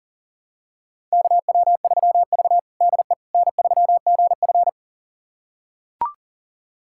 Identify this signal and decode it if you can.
Morse code, heard as a keyed tone, about 40 wpm, 700 Hz. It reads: KW3V DE N3GF
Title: USA Callsign DE Callsign - 4 Characters - 40wpm